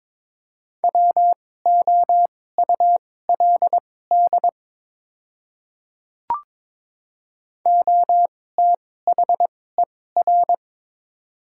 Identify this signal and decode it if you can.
Morse code, heard as a keyed tone, about 22 wpm, spaced as for 16 wpm, 690 Hz. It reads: WOULD OTHER
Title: 2x Character Spacing - Top 100 Words - 22wpm